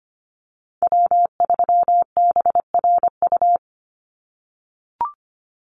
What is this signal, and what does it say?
Morse code, heard as a keyed tone, about 25 wpm, 700 Hz. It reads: W3BRU
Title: USA Call Signs 25wpm